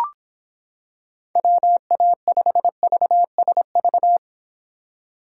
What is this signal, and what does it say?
Morse code, heard as a keyed tone, about 25 wpm, 700 Hz. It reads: WA5VSV